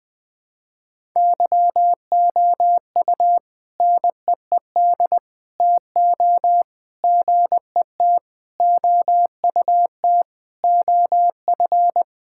Morse code, heard as a keyed tone, about 20 wpm, 700 Hz. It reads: YOU NEED TO GET OUT OF
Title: Sentences with 3-Word Phrases - Spoken Beginning - 20wpm